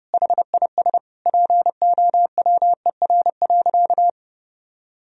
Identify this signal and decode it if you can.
Morse code, heard as a keyed tone, about 30 wpm, 700 Hz. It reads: HIS POWER.
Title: Sentences from Top 1000 Words - Review Part 2 - 30wpm